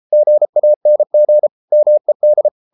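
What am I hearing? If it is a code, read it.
Morse code, heard as a keyed tone, about 33 wpm, 600 Hz. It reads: GANG MED
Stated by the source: Morse code at 33 wpm, QRQ.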